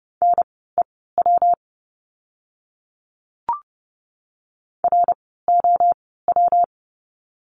Morse code, heard as a keyed tone, about 30 wpm, spaced as for 17 wpm, 700 Hz. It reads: NEW ROW
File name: course / 30:10wpm / 115 Lesson 19f - Letter w - Words 30wpm